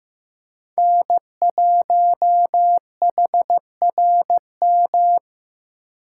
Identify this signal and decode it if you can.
Morse code, heard as a keyed tone, about 18 wpm, 700 Hz. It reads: N1HRM
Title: Speed Racing - USA 1x3 Call Signs 15wpm